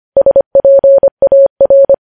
Morse code, heard as a keyed tone, about 25 wpm, 550 Hz. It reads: SPAR
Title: News Headlines at 25